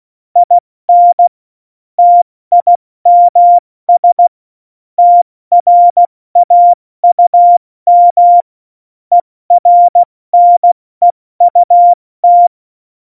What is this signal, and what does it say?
Morse code, heard as a keyed tone, about 16 wpm, spaced as for 14 wpm, 700 Hz. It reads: IN TIMS TRAUM ERNEUT
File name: Morsekurs / Witze / Witz 28. November 2017, 70 F80